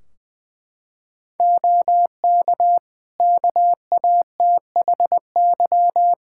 Morse code, heard as a keyed tone, about 20 wpm, 700 Hz. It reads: OK KATHY
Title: Practice QSO - Exchange 3 - More Info - 20wpm